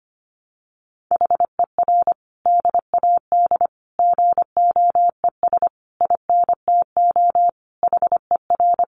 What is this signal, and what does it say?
Morse code, heard as a keyed tone, about 25 wpm, 700 Hz. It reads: HER DAD GOES INTO HER
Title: Additional Easy Phrases 25wpm